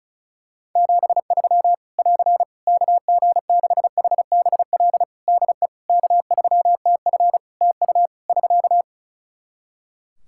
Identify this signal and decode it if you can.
Morse code, heard as a keyed tone, about 34 wpm, 700 Hz. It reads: 73 <AR> KG6HBL DE K3TF TU <SK>
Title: Practice QSO - Exchange 4 - Ending - 35wpm